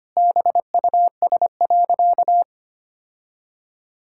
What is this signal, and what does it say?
Morse code, heard as a keyed tone, about 25 wpm, 700 Hz. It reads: BUS.
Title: Sentences from Top 1000 Words - Review Part 6 - 25wpm